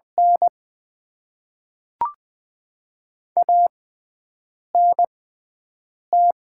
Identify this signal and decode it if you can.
Morse code, heard as a keyed tone, about 20 wpm, spaced as for 7 wpm, 700 Hz. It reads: N ANT
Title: Top 20 QSO Elements - 6x Character Spacing - 20wpm